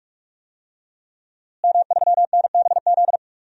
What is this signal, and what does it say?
Morse code, heard as a keyed tone, about 42 wpm, 700 Hz. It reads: M3NBZ
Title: DXCC-100 Call Signs - 5 Characters - Repeated 2x - 45wpm